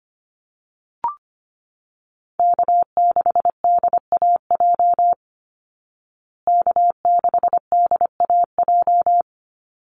Morse code, heard as a keyed tone, about 25 wpm, 700 Hz. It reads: K6DAJ K6DAJ
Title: Call Signs Repeated 2x - 25wpm